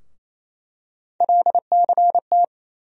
Morse code, heard as a keyed tone, about 28 wpm, 700 Hz. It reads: LCT
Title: USA Call Sign Postfixes - 3 Letters 28wpm